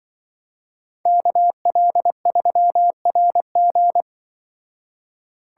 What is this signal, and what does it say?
Morse code, heard as a keyed tone, about 24 wpm, 700 Hz. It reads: KL3RG
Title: Speed Racing - USA 2x2 Call Signs 20wpm